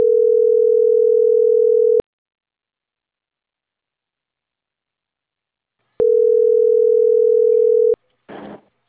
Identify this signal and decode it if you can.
Ringback tone (North America), 440 + 480 Hz.